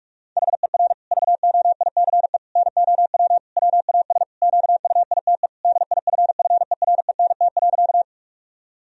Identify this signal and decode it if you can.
Morse code, heard as a keyed tone, about 47 wpm, 700 Hz. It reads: HER VOICE NOW WAS QUITE DIFFERENT.
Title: Sentences from Top 1000 Words - Review Part 3 - 45wpm